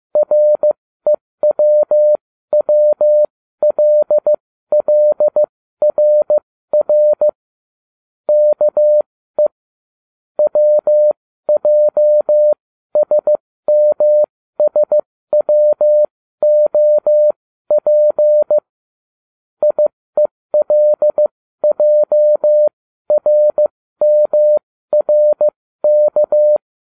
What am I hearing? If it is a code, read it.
Morse code, heard as a keyed tone, about 15 wpm, spaced as for 13 wpm, 600 Hz. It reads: REWWLLRR KE WJSMSWOP IELJRMRK